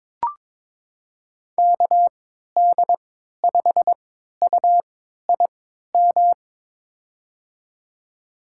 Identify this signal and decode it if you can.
Morse code, heard as a keyed tone, about 22 wpm, spaced as for 13 wpm, 700 Hz. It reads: KD5UIM
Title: USA Call Signs - 6 Characters - 3x Character Spacing - 22wpm